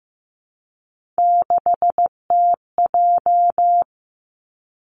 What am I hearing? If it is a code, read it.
Morse code, heard as a keyed tone, about 15 wpm, 700 Hz. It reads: 6TJ
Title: Three Challenging Letter-Number Combinations 15wpm